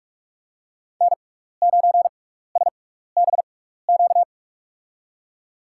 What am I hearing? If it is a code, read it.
Morse code, heard as a keyed tone, about 44 wpm, spaced as for 15 wpm, 700 Hz. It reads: N9SBQ
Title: USA Call Signs - 5 Characters - 6x Character Spacing - 45wpm